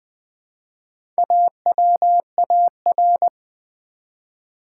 Morse code, heard as a keyed tone, about 20 wpm, 700 Hz. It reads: AWAR
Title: Build A Word - 1 Letter At A Time - 5-Letter Words - 20wpm